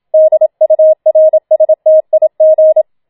Morse code, heard as a keyed tone, about 27 wpm, 630 Hz. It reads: DURSTIG